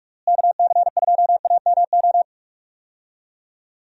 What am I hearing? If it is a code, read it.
Morse code, heard as a keyed tone, about 46 wpm, 700 Hz. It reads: KC2AMO
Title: USA Call Signs - 6 Characters - Repeated 2x - 45wpm